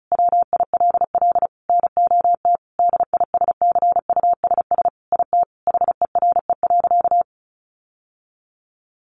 Morse code, heard as a keyed tone, about 34 wpm, 700 Hz. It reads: WILL NOT DISCUSS IT HERE.
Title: Sentences from Top 1100 Words 35wpm